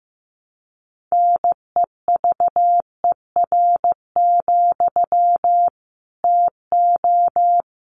Morse code, heard as a keyed tone, about 15 wpm, 700 Hz. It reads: NEVER, TO
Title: Sentences from Top 900 Words 15wpm